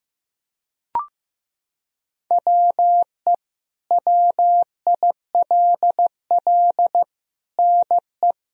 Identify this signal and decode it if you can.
Morse code, heard as a keyed tone, about 15 wpm, 700 Hz. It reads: WE WILL NE
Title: Sentences from Top 1000 Words - Review Part 4 - 15wpm